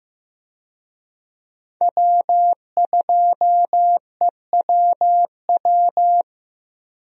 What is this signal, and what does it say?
Morse code, heard as a keyed tone, about 15 wpm, 700 Hz. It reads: W2EWW
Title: USA Call Signs - 5 Characters - Repeated 2x - 15wpm